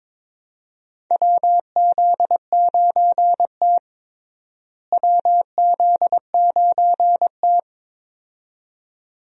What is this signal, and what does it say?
Morse code, heard as a keyed tone, about 22 wpm, 700 Hz. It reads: WZ9T WZ9T
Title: USA Call Signs - 4 Characters - Repeated 2x - 22wpm